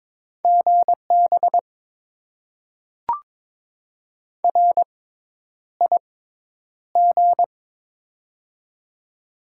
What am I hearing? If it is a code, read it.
Morse code, heard as a keyed tone, about 22 wpm, spaced as for 10 wpm, 700 Hz. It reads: GB RIG
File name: Top 90 QSO Elements - 6x Character Spacing - 22wpm